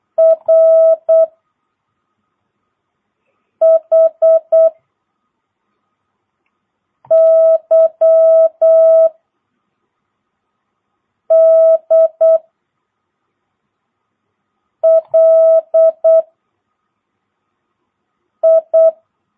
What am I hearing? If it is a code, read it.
Morse code, heard as a keyed tone, about 8 wpm, 1300 Hz. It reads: R H Y D L I
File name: jh8cbh / 2022cw